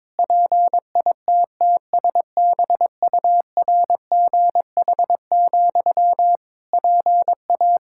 Morse code, heard as a keyed tone, about 21 wpm, 700 Hz. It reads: PITTSBURGH, PA
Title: Practice QSO - Exchange 1 - Beginning - 22wpm